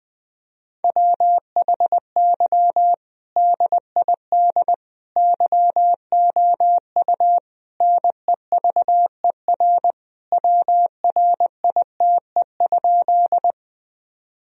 Morse code, heard as a keyed tone, about 21 wpm, 700 Hz. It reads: WHY DID YOU NEVER WRITE?
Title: Sentences from Top 1000 Words - Review Part 1 - 20wpm